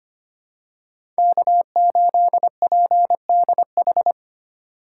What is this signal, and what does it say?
Morse code, heard as a keyed tone, about 25 wpm, 700 Hz. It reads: K8PDH
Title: USA Call Signs - 5 Characters - 4x Character Spacing - 25wpm